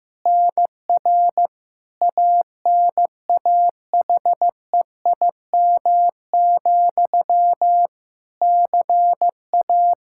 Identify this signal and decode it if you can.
Morse code, heard as a keyed tone, about 15 wpm, 700 Hz. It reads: NR ANAHEIM, CA
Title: Practice QSO - Exchange 1 - Beginning - 15wpm